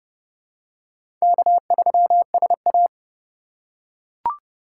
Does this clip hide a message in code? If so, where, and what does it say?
Yes: Morse code from 1.2 to 2.9 s, heard as a keyed tone, about 29 wpm, 700 Hz. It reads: K3SA